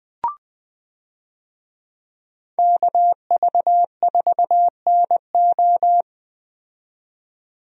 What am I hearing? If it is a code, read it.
Morse code, heard as a keyed tone, about 20 wpm, 700 Hz. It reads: KV4NO
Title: USA Call Signs - 5 Characters - 20wpm